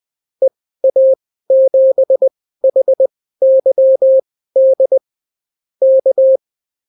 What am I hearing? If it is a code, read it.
Morse code, heard as a keyed tone, about 20 wpm, spaced as for 14 wpm, 530 Hz. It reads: EA7HYD K
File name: CW QSO libroCW4 20-10ppm